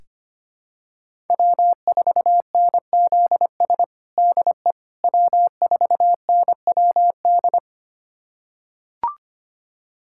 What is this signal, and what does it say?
Morse code, heard as a keyed tone, about 25 wpm, 700 Hz. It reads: W4NZS DE W4NWD
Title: USA Callsign DE Callsign - 5 Characters - 25wpm